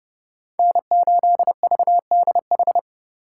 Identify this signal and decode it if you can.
Morse code, heard as a keyed tone, about 30 wpm, 700 Hz. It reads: N8VDH
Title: USA Call Signs - 5 Characters - 3x Character Spacing - 30wpm